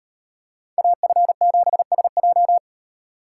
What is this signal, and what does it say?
Morse code, heard as a keyed tone, about 36 wpm, 700 Hz. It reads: AF7SJ